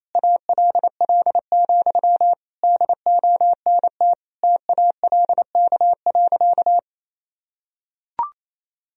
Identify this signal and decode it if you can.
Morse code, heard as a keyed tone, about 28 wpm, 700 Hz. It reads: ALL, DONT TALK.
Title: Sentences from Top 1000 Words - Review Part 2 - 28wpm